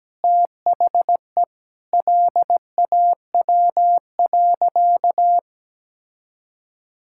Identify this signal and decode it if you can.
Morse code, heard as a keyed tone, about 17 wpm, 700 Hz. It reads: THE LAW.
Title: Sentences from Top 1000 Words - Review Part 6 - 17wpm